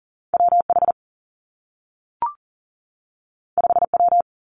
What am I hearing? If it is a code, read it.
Morse code, heard as a keyed tone, about 39 wpm, 700 Hz. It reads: WH 5W
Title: Two Challenging Letter-Number Combinations 40wpm